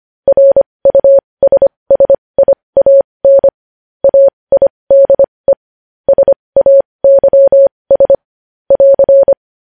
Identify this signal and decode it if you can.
Morse code, heard as a keyed tone, about 25 wpm, spaced as for 20 wpm, 550 Hz. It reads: RUSSIAN AIDE SAYS <AR>
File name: podcast / News headlines-1x-25x20